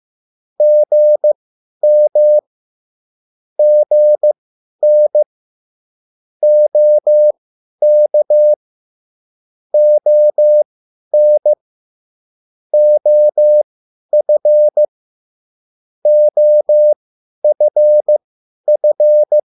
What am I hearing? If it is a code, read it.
Morse code, heard as a keyed tone, about 15 wpm, spaced as for 10 wpm, 600 Hz. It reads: GM GN OK ON OF OFF
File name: cwbook